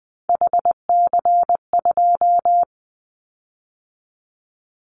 Morse code, heard as a keyed tone, about 19 wpm, 700 Hz. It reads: HC2
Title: Callsign Prefixes for Top 100 DXCC - 20wpm